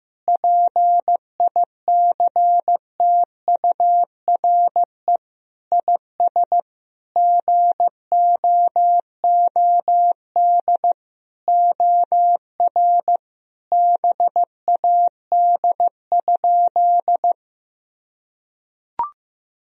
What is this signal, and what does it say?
Morse code, heard as a keyed tone, about 16 wpm, 700 Hz. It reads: PICTURE IS GOOD OR BAD?
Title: Sentences from Top 1000 Words - Review Part 1 - 15wpm